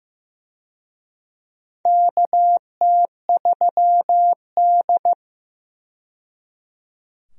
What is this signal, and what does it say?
Morse code, heard as a keyed tone, about 15 wpm, 700 Hz. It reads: KT3D
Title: USA Call Signs - 4 Characters - 15wpm